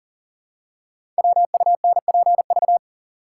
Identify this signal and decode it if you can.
Morse code, heard as a keyed tone, about 38 wpm, 690 Hz. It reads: WUNPV
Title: Code Groups - 5 Letters - 40wpm